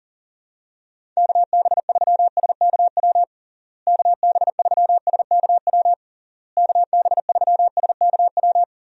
Morse code, heard as a keyed tone, about 37 wpm, 700 Hz. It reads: KB3SKW KB3SKW KB3SKW
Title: USA Call Signs - 6 Characters - Repeated 3x - 40wpm